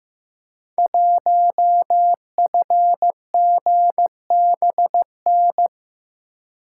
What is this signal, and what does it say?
Morse code, heard as a keyed tone, about 15 wpm, 700 Hz. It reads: 1FGBN
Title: Code Groups - 5 Letter:Numbers - 15wpm